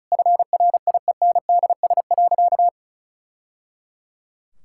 Morse code, heard as a keyed tone, about 34 wpm, 700 Hz. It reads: FRIENDS.
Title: Sentences from Top 1000 Words - Review Part 5 - 35wpm